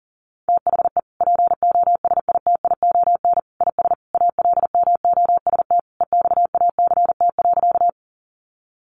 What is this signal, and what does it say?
Morse code, heard as a keyed tone, about 37 wpm, 700 Hz. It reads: THE POSITION IS ALMOST EXACT.
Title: Sentences from Top 1300 Words 40wpm